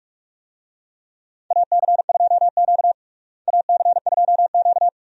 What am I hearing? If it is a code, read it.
Morse code, heard as a keyed tone, about 45 wpm, 700 Hz. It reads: AC2Q AC2Q
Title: USA Call Signs - 4 Characters - Repeated 2x - 45wpm